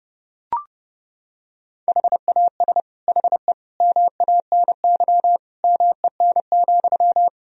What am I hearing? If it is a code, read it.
Morse code, heard as a keyed tone, about 30 wpm, 700 Hz. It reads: HAS HE MANY MEN,
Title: Sentences from Top 1000 Words - Review Part 1 - 30wpm